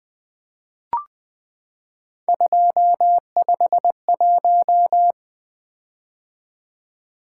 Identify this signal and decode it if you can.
Morse code, heard as a keyed tone, about 20 wpm, 700 Hz. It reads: 251
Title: Three Letter-Number Combinations - 20wpm